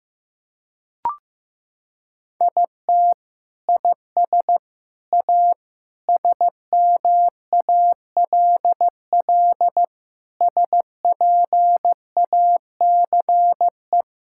Morse code, heard as a keyed tone, about 15 wpm, 700 Hz. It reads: IT IS A SMALL SPACE
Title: Sentences from Top 1000 Words - Review Part 5 - 15wpm